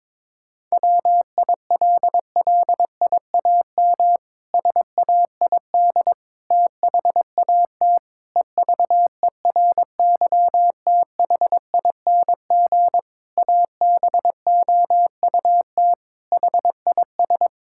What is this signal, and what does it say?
Morse code, heard as a keyed tone, about 22 wpm, 700 Hz. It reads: WILLIAM SAID THAT EVERYTHING ABOUT HIS
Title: Pangram Challenge v2 - 22wpm